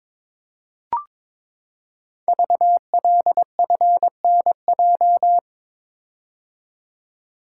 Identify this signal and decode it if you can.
Morse code, heard as a keyed tone, about 22 wpm, 700 Hz. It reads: VLFNJ
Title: Code Groups - 5 Letters - 22wpm